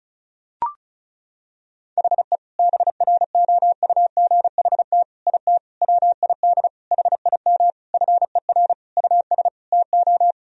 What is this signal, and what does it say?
Morse code, heard as a keyed tone, about 36 wpm, 700 Hz. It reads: HE BROUGHT IT WID HIM FER US TO
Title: Sentences from Top 500 Words and Abbreviations 35wpm